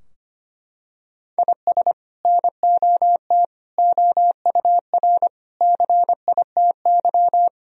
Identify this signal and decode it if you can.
Morse code, heard as a keyed tone, about 25 wpm, 700 Hz. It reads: IS NOT OUR CITY
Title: Sentences from Top 1000 Words - Review Part 3 - 25wpm